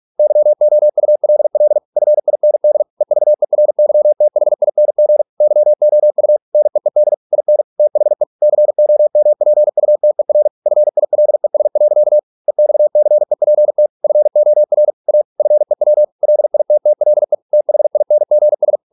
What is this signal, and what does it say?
Morse code, heard as a keyed tone, about 46 wpm, 600 Hz. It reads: YOULL FIND EVERYTHING YOU NEED IN THE COMPUTER FILES. EXCEPT FOR A FEW LITTLE THINGS